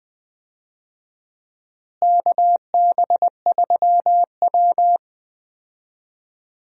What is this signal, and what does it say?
Morse code, heard as a keyed tone, about 20 wpm, 700 Hz. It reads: KB3W
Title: USA Call Signs - 4 Characters - Repeated 2x - 20wpm